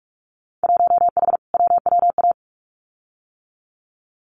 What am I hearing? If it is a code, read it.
Morse code, heard as a keyed tone, about 46 wpm, 700 Hz. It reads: 1H WWA
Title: Winter Field Day Exchanges 45wpm